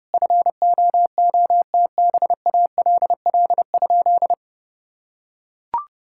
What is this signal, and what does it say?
Morse code, heard as a keyed tone, about 30 wpm, 700 Hz. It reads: FOOTBALL?
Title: Sentences from Top 1600 Words 30wpm